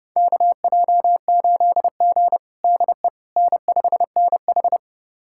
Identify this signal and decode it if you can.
Morse code, heard as a keyed tone, about 30 wpm, 700 Hz. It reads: KJ8G DE N5NH
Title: USA Callsign DE Callsign - 4 Characters - 30wpm